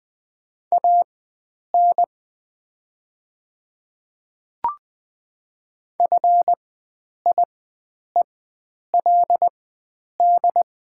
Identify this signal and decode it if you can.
Morse code, heard as a keyed tone, about 20 wpm, spaced as for 9 wpm, 700 Hz. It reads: AN FIELD
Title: 4x Character Spacing - Top 500 Words - 20wpm